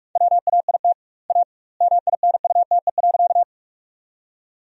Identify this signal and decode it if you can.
Morse code, heard as a keyed tone, about 47 wpm, 700 Hz. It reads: WAIT A MINUTE.
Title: 2x Word-Spacing - Sentences from Top 500 Words - 45wpm